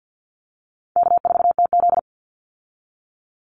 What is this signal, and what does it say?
Morse code, heard as a keyed tone, about 50 wpm, 700 Hz. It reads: K4TZ
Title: Easy Call Signs Repeated 3x - 50wpm